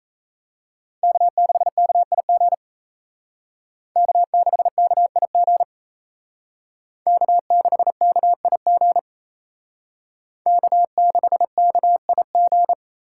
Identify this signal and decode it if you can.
Morse code, heard as a keyed tone, about 35 wpm, 700 Hz. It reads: K6KIG K6KIG K6KIG K6KIG
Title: Speed Racing - USA 1x3 Call Signs 28wpm